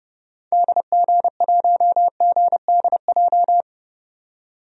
Morse code, heard as a keyed tone, about 31 wpm, 700 Hz. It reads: DG1GDJ
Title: DXCC-100 Call Signs - 6 Characters - 30wpm